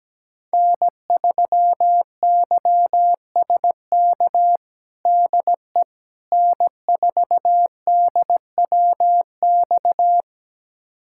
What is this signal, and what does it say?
Morse code, heard as a keyed tone, about 17 wpm, 700 Hz. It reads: N3YSK DE N4DWX
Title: USA Callsign DE Callsign - 5 Characters - 17wpm